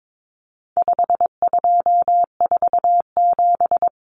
Morse code, heard as a keyed tone, about 22 wpm, 700 Hz. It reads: 5247
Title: Common Cut Numbers - 4-Digit 22wpm